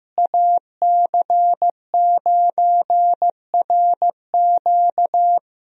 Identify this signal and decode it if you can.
Morse code, heard as a keyed tone, about 15 wpm, 700 Hz. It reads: AC9RQ
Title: USA Call Signs - 5 Characters - Repeated 2x - 15wpm